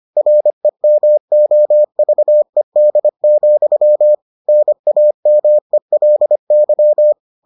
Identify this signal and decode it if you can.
Morse code, heard as a keyed tone, about 25 wpm, 600 Hz. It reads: REMOVED, NAMELY